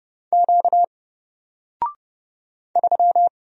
Morse code, heard as a keyed tone, about 31 wpm, 700 Hz. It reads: Q 3
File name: Single-Letter-Number-30wpm